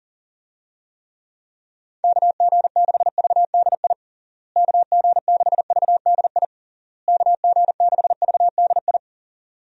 Morse code, heard as a keyed tone, about 40 wpm, 700 Hz. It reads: KG6VDI KG6VDI KG6VDI
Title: USA Call Signs - 6 Characters - Repeated 3x - 40wpm